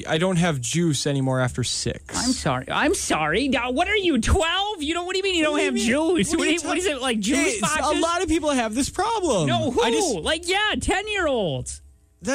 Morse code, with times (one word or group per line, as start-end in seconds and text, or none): none